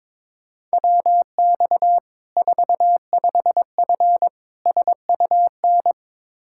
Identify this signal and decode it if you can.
Morse code, heard as a keyed tone, about 22 wpm, 700 Hz. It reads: WX 45F SUN
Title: Practice QSO - Exchange 2 - Info - 22wpm